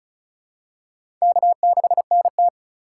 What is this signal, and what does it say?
Morse code, heard as a keyed tone, about 34 wpm, 700 Hz. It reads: K6NT
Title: USA Call Signs - 4 Characters - 4x Character Spacing - 35wpm